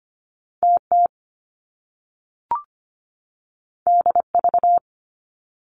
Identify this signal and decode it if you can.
Morse code, heard as a keyed tone, about 25 wpm, 700 Hz. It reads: TT DV
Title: Top Two Letter Combinations in Words 25wpm